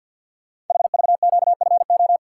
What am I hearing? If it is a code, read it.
Morse code, heard as a keyed tone, about 52 wpm, 700 Hz. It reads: HVQFO
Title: Code Groups - 5 Letters - 50wpm